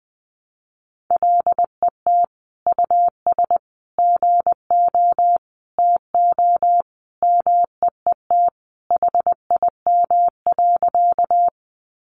Morse code, heard as a keyed tone, about 20 wpm, 700 Hz. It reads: LET US GO TO MEET HIM.
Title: Sentences from Top 1100 Words 20wpm